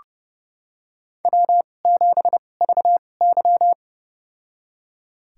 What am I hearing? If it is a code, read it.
Morse code, heard as a keyed tone, about 30 wpm, spaced as for 22 wpm, 700 Hz. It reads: W7VY
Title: USA Call Signs - 4 Characters - 2x Character Spacing - 30wpm